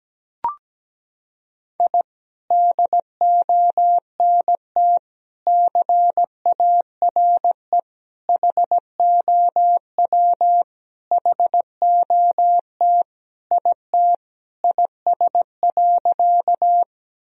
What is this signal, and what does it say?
Morse code, heard as a keyed tone, about 17 wpm, 700 Hz. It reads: I DONT CARE HOW HOT IT IS.
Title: Sentences from Top 1000 Words - Review Part 3 - 17wpm